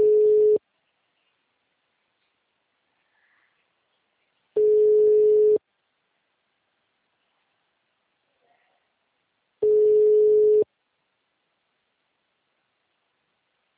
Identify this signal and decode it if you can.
Ringback tone (Europe, 425 Hz).